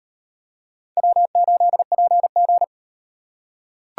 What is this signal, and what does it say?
Morse code, heard as a keyed tone, about 37 wpm, 700 Hz. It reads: W8PG